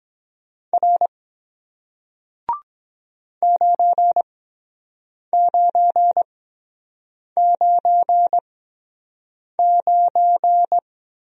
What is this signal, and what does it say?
Morse code, heard as a keyed tone, about 23 wpm, spaced as for 7 wpm, 700 Hz. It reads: R 9999